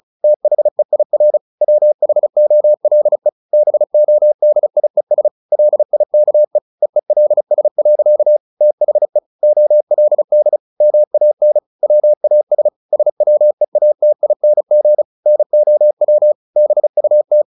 Morse code, heard as a keyed tone, about 35 wpm, 600 Hz. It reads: THEIR WHOLE BODIES LIKE EELS. THE OLD MAN WAS SWEATING NOW BUT
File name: the old man and the sea 2 35wpm